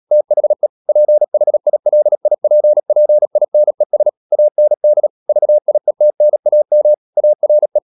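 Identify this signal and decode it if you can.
Morse code, heard as a keyed tone, about 37 wpm, 600 Hz. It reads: THE PHILIPPINES AND VIETNAM ARE
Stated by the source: Morse code practice at 35–39 wpm sending English text of Ken Follett's novel Never, from CW med Gnister.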